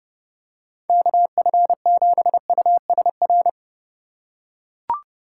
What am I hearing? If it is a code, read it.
Morse code, heard as a keyed tone, about 31 wpm, 700 Hz. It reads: KF7USR